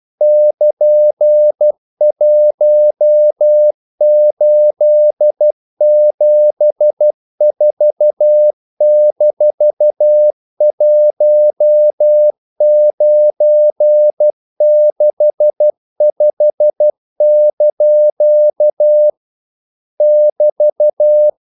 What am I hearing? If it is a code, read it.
Morse code, heard as a keyed tone, about 12 wpm, 600 Hz. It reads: <KN>1874-1965) <BT>
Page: English quotations in CW (telegraphy - morse)